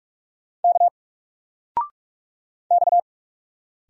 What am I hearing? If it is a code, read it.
Morse code, heard as a keyed tone, about 44 wpm, 700 Hz. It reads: K X
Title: Single Letter-Number 45wpm